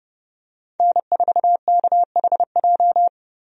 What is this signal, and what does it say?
Morse code, heard as a keyed tone, about 30 wpm, 700 Hz. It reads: N4KHJ
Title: USA Call Signs - 5 Characters - Repeated 2x - 30wpm